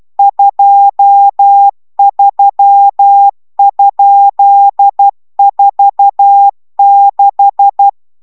Morse code, heard as a keyed tone, about 12 wpm, 800 Hz. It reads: 23?46